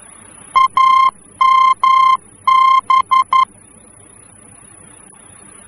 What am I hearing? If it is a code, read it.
Morse code, heard as a keyed tone, about 11 wpm, 1050 Hz. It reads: AMB
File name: france / mx / AMB-F-VOR